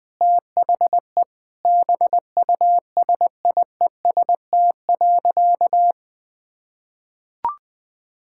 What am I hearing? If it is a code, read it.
Morse code, heard as a keyed tone, about 21 wpm, 700 Hz. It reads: THE BUSIEST.
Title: Sentences with 7-Word Phrases - Spoken Beginning - 20wpm